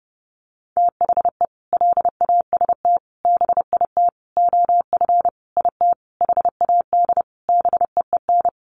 Morse code, heard as a keyed tone, about 28 wpm, 700 Hz. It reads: THE LAST BIT OF IT HAD BEEN
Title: Sentences from Top 900 Words 30wpm